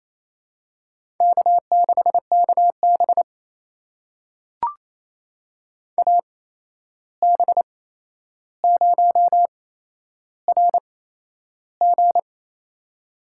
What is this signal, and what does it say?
Morse code, heard as a keyed tone, about 28 wpm, spaced as for 11 wpm, 700 Hz. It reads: K6KB AB0RG
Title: USA Call Signs - 8x Character Spacing - 28wpm